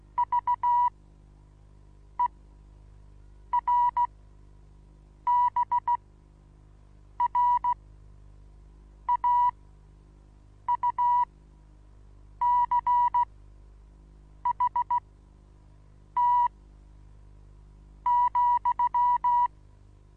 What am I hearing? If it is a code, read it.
Morse code, heard as a keyed tone, about 16 wpm, spaced as for 6 wpm, 970 Hz. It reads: VERBRAUCHT,
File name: Onlinelehrgang / morsen / DJ4UF Morsen25